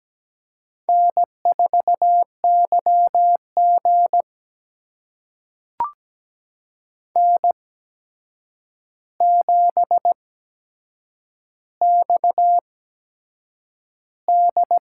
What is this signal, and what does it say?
Morse code, heard as a keyed tone, about 17 wpm, spaced as for 7 wpm, 700 Hz. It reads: N4YG N7XD
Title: USA Call Signs - 4 Characters - 8x Character Spacing - 17wpm